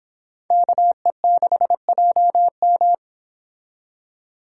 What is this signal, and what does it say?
Morse code, heard as a keyed tone, about 25 wpm, 700 Hz. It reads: KE6JM